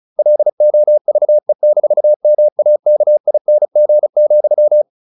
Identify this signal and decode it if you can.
Morse code, heard as a keyed tone, about 34 wpm, 600 Hz. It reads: LOVE-MAKING,
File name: the old man and the sea / the old man and the sea 1 35wpm